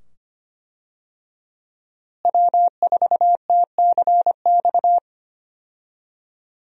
Morse code, heard as a keyed tone, about 25 wpm, 700 Hz. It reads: W4TCX
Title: USA Call Signs - 5 Characters - Repeated 2x - 25wpm